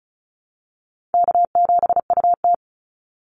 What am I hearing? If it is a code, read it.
Morse code, heard as a keyed tone, about 34 wpm, 700 Hz. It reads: K7UT
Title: Easy Call Signs Repeated 3x - 35wpm